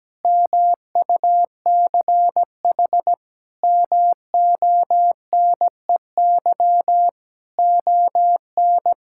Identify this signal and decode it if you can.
Morse code, heard as a keyed tone, about 17 wpm, 700 Hz. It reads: MUCH MONEY ON
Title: Sentences from Top 1000 Words - Review Part 5 - 17wpm